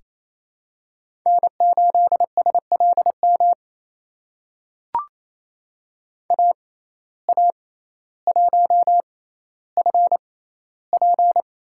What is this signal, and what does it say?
Morse code, heard as a keyed tone, about 28 wpm, spaced as for 14 wpm, 700 Hz. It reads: N8SLM AA1FP
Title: USA Call Signs - 5 Characters - 6x Character Spacing - 28wpm